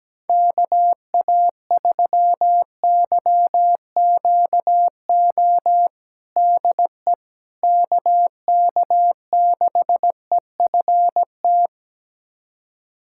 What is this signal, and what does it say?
Morse code, heard as a keyed tone, about 17 wpm, 700 Hz. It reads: KA3YQO DE KK6EFT
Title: USA Callsign DE Callsign - 6 Characters - 17wpm